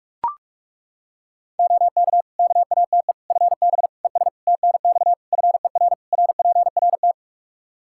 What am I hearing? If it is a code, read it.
Morse code, heard as a keyed tone, about 41 wpm, 700 Hz. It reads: OK KATE FB ES TNX FER RPRT
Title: Practice QSO - Exchange 2 - Info - 45wpm